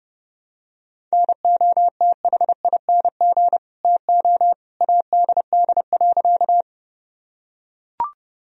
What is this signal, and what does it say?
Morse code, heard as a keyed tone, about 32 wpm, 700 Hz. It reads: NOTHING TO ADD.
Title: Sentences from Top 1000 Words - Review Part 1 - 30wpm